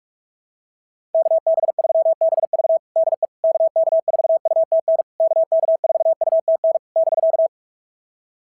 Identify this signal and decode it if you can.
Morse code, heard as a keyed tone, about 46 wpm, 650 Hz. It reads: KB3BV DE KK4UTN KK4UTN <BK>
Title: SOTA QSO Practice 45wpm